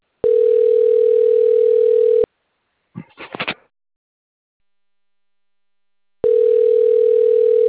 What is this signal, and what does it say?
Ringback tone (North America), 440 + 480 Hz.